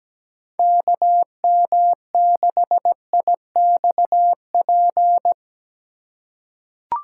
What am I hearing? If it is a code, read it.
Morse code, heard as a keyed tone, about 17 wpm, 700 Hz. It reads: KM6IXP
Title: USA Call Signs - 4x Character Spacing - 17wpm